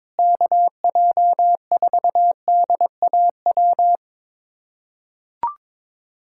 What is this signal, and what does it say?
Morse code, heard as a keyed tone, about 22 wpm, 700 Hz. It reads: KJ4DAW
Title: USA Callsign DE Callsign - 6 Characters - 22wpm